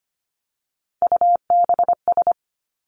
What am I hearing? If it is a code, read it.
Morse code, heard as a keyed tone, about 25 wpm, 700 Hz. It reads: UBS
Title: Three Challenging Letter-Number Combinations 25wpm